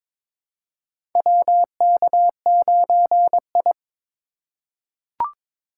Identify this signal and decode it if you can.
Morse code, heard as a keyed tone, about 22 wpm, 700 Hz. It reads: WK9I
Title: USA Call Signs - 4x Character Spacing - 22wpm